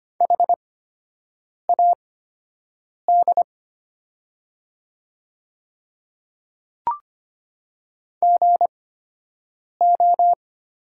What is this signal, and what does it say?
Morse code, heard as a keyed tone, about 25 wpm, spaced as for 7 wpm, 700 Hz. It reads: HAD GO
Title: 8x Character Spacing - Top 200 Words - 25wpm